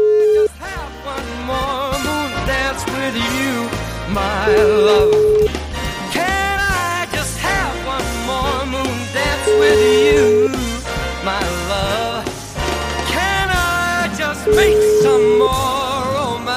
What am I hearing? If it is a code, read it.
Ringback tone (Europe, 425 Hz), with speech.